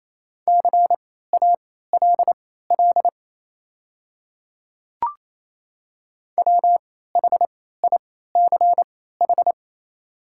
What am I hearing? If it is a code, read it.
Morse code, heard as a keyed tone, about 27 wpm, spaced as for 16 wpm, 700 Hz. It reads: CALL WHICH
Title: 3x Character Spacing - Top 300 Words - 28wpm